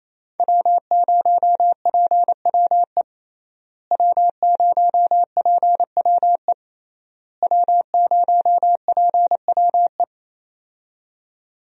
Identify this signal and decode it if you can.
Morse code, heard as a keyed tone, about 28 wpm, 700 Hz. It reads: W0PWE W0PWE W0PWE
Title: USA Call Signs - 5 Characters - Repeated 3x - 28wpm